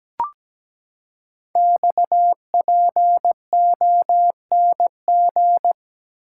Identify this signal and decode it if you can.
Morse code, heard as a keyed tone, about 17 wpm, 700 Hz. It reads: XPONG
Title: Code Groups - 5 Letters - 17wpm